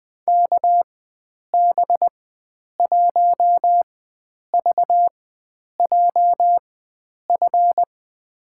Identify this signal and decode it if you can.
Morse code, heard as a keyed tone, about 20 wpm, spaced as for 9 wpm, 700 Hz. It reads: KB1VJF
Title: USA Call Signs - 6 Characters - 4x Character Spacing - 20wpm